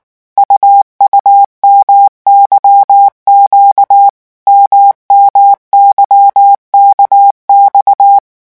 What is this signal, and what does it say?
Morse code, heard as a keyed tone, about 19 wpm, 800 Hz. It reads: UUMYQ MMYKX
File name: morse mp3 / revision2-vitesse 18 mots